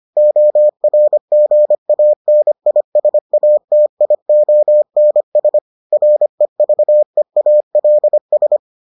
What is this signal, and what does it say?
Morse code, heard as a keyed tone, about 25 wpm, 600 Hz. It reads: ORGANISATIONS REVEALS